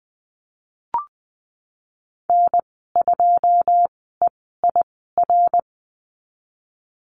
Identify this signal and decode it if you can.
Morse code, heard as a keyed tone, about 20 wpm, spaced as for 15 wpm, 700 Hz. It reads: N2EIR
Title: 082 Lesson 14g - Number 2 - Call Signs 20wpm